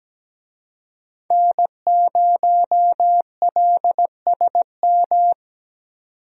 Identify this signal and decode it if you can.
Morse code, heard as a keyed tone, about 17 wpm, 700 Hz. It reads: N0LSM
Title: USA Call Signs - 5 Characters - 17wpm